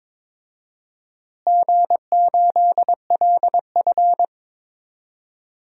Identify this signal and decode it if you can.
Morse code, heard as a keyed tone, about 22 wpm, 700 Hz. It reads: G8LF
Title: DXCC-100 Call Signs - 4 Characters - Repeated 3x - 22wpm